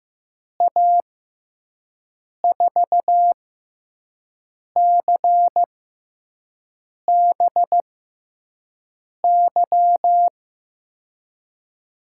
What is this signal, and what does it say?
Morse code, heard as a keyed tone, about 15 wpm, spaced as for 5 wpm, 700 Hz. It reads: A4CBY
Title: USA Call Signs - 6 Characters - 6x Character Spacing - 15wpm